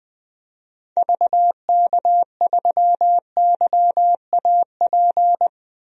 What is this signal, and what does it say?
Morse code, heard as a keyed tone, about 21 wpm, 700 Hz. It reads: VK3YAP
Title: DXCC-100 Call Signs - 6 Characters - 20wpm